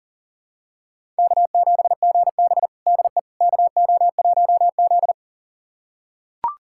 Morse code, heard as a keyed tone, about 40 wpm, 700 Hz. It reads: K7GB DE KO1Z
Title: USA Callsign DE Callsign - 4 Characters - 40wpm